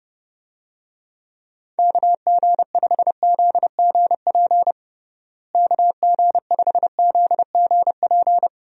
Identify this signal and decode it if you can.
Morse code, heard as a keyed tone, about 30 wpm, 700 Hz. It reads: KG5ZGP KG5ZGP
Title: USA Call Signs - 6 Characters - Repeated 3x - 30wpm